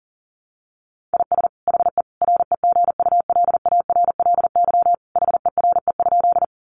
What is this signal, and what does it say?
Morse code, heard as a keyed tone, about 40 wpm, 700 Hz. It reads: IS HE REGULARLY HERE?
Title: Sentences from Top 1400 Words 40wpm